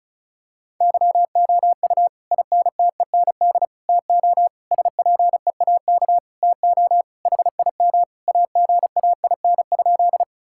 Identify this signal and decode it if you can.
Morse code, heard as a keyed tone, about 36 wpm, 700 Hz. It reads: YOU INTEND TO SPEAK TO HIM AGAIN?
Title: Sentences from Top 1600 Words 35wpm